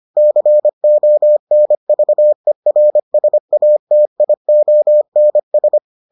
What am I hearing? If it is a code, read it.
Morse code, heard as a keyed tone, about 25 wpm, 600 Hz. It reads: CONVERSATIONS